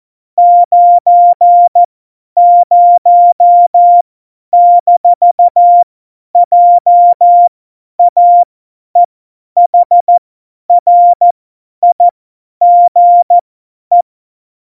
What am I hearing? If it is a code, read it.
Morse code, heard as a keyed tone, about 14 wpm, spaced as for 10 wpm, 700 Hz. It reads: 90-JAEHRIGE
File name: Morsekurs / Witze / Witz 08. Dezember 2020, 50 F70